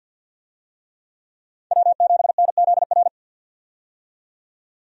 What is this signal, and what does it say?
Morse code, heard as a keyed tone, about 49 wpm, 700 Hz. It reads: W7NZR
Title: USA Call Signs - 5 Characters - 50wpm